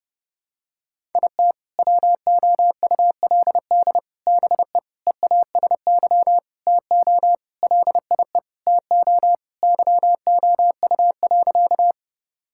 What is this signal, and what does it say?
Morse code, heard as a keyed tone, about 30 wpm, 700 Hz. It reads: IT WOULD BE EASY TO LIE TO YOU.
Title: Sentences from Top 1600 Words 30wpm